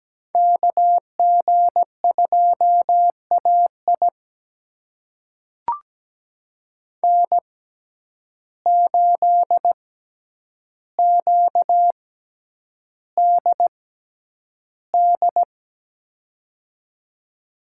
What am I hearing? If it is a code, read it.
Morse code, heard as a keyed tone, about 17 wpm, spaced as for 9 wpm, 700 Hz. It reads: KG2AI N8QDD
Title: USA Call Signs - 5 Characters - 6x Character Spacing - 17wpm